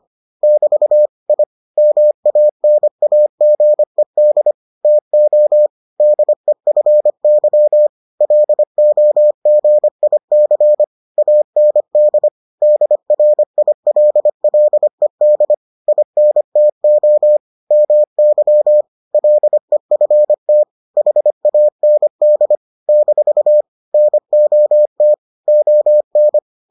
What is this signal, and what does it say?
Morse code, heard as a keyed tone, about 25 wpm, 600 Hz. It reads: <BT> I MANAGED TO DEFY LOGIC AND DRILLED INTO MY LEFT HAND - NOT ON